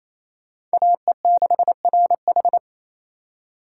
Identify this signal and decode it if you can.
Morse code, heard as a keyed tone, about 35 wpm, 700 Hz. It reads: AE6RH
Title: Speed Racing - USA 2x2 Call Signs 28wpm